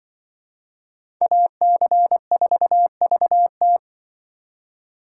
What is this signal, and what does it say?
Morse code, heard as a keyed tone, about 24 wpm, 700 Hz. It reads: AC4VT
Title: Speed Racing - USA 2x2 Call Signs 20wpm